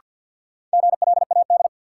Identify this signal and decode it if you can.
Morse code, heard as a keyed tone, about 52 wpm, 700 Hz. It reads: GLAD
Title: Sentences from Top 800 Words - New Word Drill - 50wpm